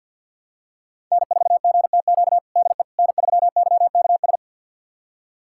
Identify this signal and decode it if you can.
Morse code, heard as a keyed tone, about 52 wpm, 700 Hz. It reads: N4GTQ DE N3YKS
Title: USA Callsign DE Callsign - 5 Characters - 50wpm